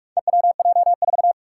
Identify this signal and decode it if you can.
Morse code, heard as a keyed tone, about 43 wpm, 700 Hz. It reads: EWJ4
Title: Four Challenging Letter-Number Combinations 45wpm